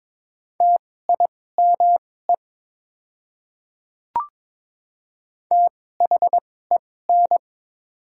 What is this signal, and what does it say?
Morse code, heard as a keyed tone, about 22 wpm, spaced as for 16 wpm, 700 Hz. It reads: TIME THEN
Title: 2x Character Spacing - Top 300 Words - 22wpm